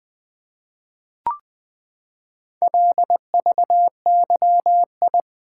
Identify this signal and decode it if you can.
Morse code, heard as a keyed tone, about 20 wpm, 700 Hz. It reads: LVYI
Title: Four Letter-Number Combinations - 20wpm